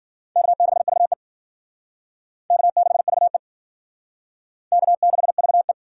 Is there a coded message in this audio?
Yes: Morse code from 0.4 to 5.7 s, heard as a keyed tone, about 53 wpm, 700 Hz. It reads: K6VE K6VE K6VE